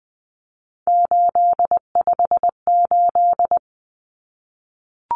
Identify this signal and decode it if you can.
Morse code, heard as a keyed tone, about 20 wpm, 700 Hz. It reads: 858
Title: Numbers - 3-Digit 20wpm